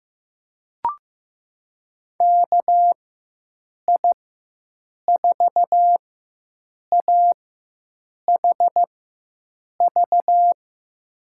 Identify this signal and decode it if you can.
Morse code, heard as a keyed tone, about 15 wpm, spaced as for 7 wpm, 700 Hz. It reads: KI4AHV
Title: USA Call Signs - 6 Characters - 4x Character Spacing - 15wpm